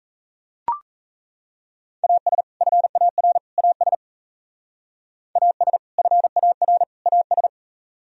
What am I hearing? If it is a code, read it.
Morse code, heard as a keyed tone, about 33 wpm, 700 Hz. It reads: AS FAR AS AS FAR AS
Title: Speed-Racing - Sets of 3 Words 28wpm